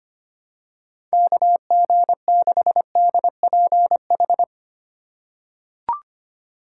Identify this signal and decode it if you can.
Morse code, heard as a keyed tone, about 25 wpm, 700 Hz. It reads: KG6DPH
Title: USA Call Signs - 6 Characters - 2x Character Spacing - 25wpm